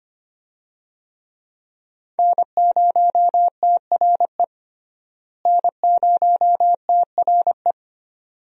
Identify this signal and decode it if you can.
Morse code, heard as a keyed tone, about 25 wpm, 700 Hz. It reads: N0TRE N0TRE
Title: USA Call Signs - 5 Characters - Repeated 2x - 25wpm